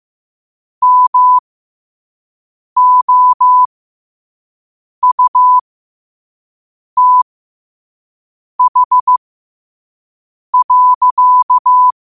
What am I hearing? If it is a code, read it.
Morse code, heard as a keyed tone, about 15 wpm, spaced as for 5 wpm, 1000 Hz. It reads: MOUTH.
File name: qotdfiles / 5wpm